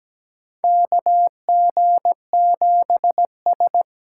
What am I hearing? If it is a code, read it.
Morse code, heard as a keyed tone, about 17 wpm, 700 Hz. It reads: KG7S
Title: USA Call Signs - 6 Characters - 8x Character Spacing - 17wpm